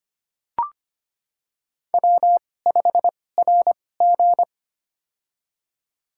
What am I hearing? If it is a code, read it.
Morse code, heard as a keyed tone, about 25 wpm, spaced as for 18 wpm, 700 Hz. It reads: W5RG
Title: USA Call Signs - 4 Characters - 2x Character Spacing - 25wpm